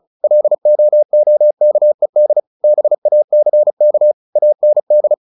Morse code, heard as a keyed tone, about 34 wpm, 600 Hz. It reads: LOOKED BACK AND
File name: the old man and the sea 2 35wpm